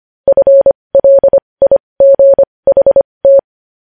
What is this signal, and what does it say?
Morse code, heard as a keyed tone, about 25 wpm, spaced as for 20 wpm, 550 Hz. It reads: FLIGHT
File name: podcast / News headlines-1x-25x20